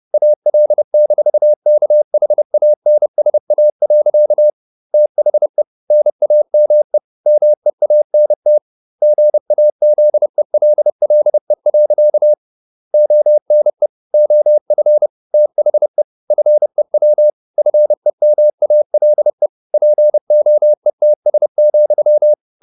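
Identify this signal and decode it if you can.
Morse code, heard as a keyed tone, about 30 wpm, 600 Hz. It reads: AL-KHANSA. THE NAME MEANT GAZELLE. ONE OF THE FEW FEMALE POETS,
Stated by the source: Morse code at 30 wpm, the novel Never by Ken Follett in English, from CW med Gnister.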